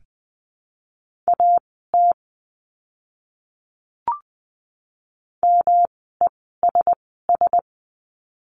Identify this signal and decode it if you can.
Morse code, heard as a keyed tone, about 20 wpm, spaced as for 14 wpm, 700 Hz. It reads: AT MESS
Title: 108 Lesson 18f - Letter m - Words 20wpm